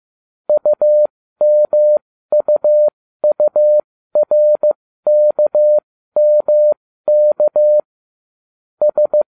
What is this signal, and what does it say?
Morse code, heard as a keyed tone, about 15 wpm, spaced as for 13 wpm, 600 Hz. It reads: UMUURKMK S